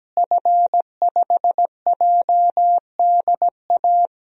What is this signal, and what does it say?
Morse code, heard as a keyed tone, about 17 wpm, 700 Hz. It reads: F5JDA
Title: DXCC-100 Call Signs - 5 Characters - Repeated 3x - 17wpm